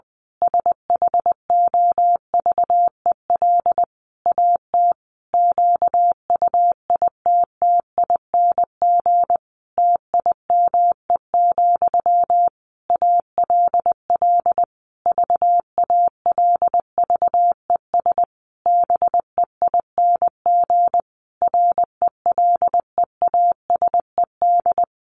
Morse code, heard as a keyed tone, about 20 wpm, 700 Hz. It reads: SHOVEL AT QUITTING TIME, ALL VALVES BEING RELEASED
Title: A Sound Of Thunder 20wpm